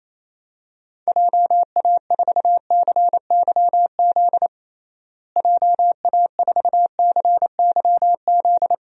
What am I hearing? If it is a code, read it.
Morse code, heard as a keyed tone, about 27 wpm, 700 Hz. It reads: JA4CYZ JA4CYZ
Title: DXCC-100 Call Signs - 6 Characters - Repeated 3x - 28wpm